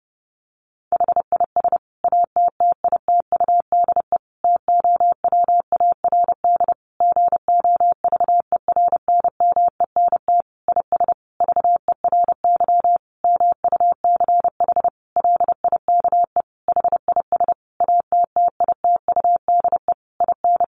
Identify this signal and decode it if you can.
Morse code, heard as a keyed tone, about 30 wpm, 700 Hz. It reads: HIS ATTITUDE TOWARD GOVERNMENT IS VERY MUCH LIKE HIS ATTITUDE IN
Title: Sentences from Top 1300 Words 30wpm